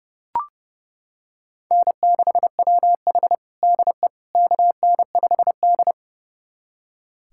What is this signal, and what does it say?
Morse code, heard as a keyed tone, about 32 wpm, 700 Hz. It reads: N6WH DE KN5D
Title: USA Callsign DE Callsign - 4 Characters - 30wpm